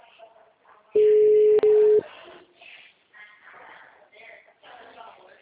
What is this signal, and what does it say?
Ringback tone (Europe, 425 Hz), with speech.
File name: out- -2054- - - .116